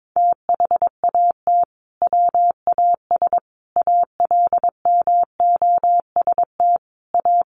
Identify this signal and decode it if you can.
Morse code, heard as a keyed tone, about 22 wpm, 700 Hz. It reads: THAT WAS ALMOST A
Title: Sentences from Top 1400 Words 22wpm